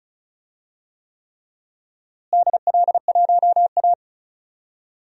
Morse code, heard as a keyed tone, about 36 wpm, 700 Hz. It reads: DL1A
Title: DXCC-100 Call Signs - 4 Characters - Repeated 3x - 35wpm